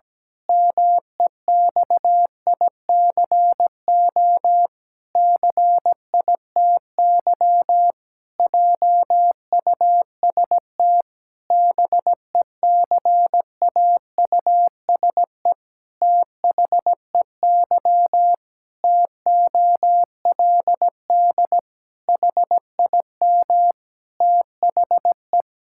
Morse code, heard as a keyed tone, about 17 wpm, 700 Hz. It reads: MEXICO CITY JUST BECAUSE THEY TOLD HIM THE
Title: Pangram Challenge v2 - 17wpm